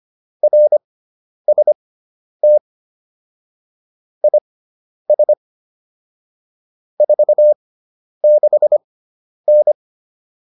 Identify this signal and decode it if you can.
Morse code, heard as a keyed tone, about 25 wpm, spaced as for 10 wpm, 600 Hz. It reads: RST IS 46N
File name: QSO306 10